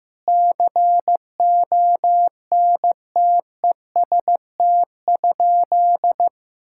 Morse code, heard as a keyed tone, about 15 wpm, 700 Hz. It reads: CONTEST?
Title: Sentences from Top 1600 Words 15wpm